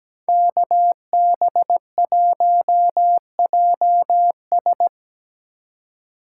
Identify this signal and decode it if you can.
Morse code, heard as a keyed tone, about 17 wpm, 700 Hz. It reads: KB1JS
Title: USA Call Signs - 5 Characters - 4x Character Spacing - 17wpm